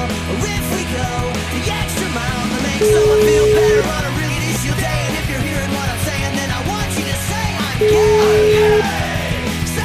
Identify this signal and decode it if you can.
Ringback tone (Europe, 425 Hz), with noise.